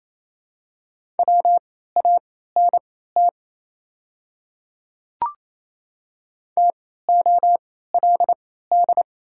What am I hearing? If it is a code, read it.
Morse code, heard as a keyed tone, about 28 wpm, spaced as for 16 wpm, 700 Hz. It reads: WANT TOLD
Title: 3x Character Spacing - Top 500 Words - 28wpm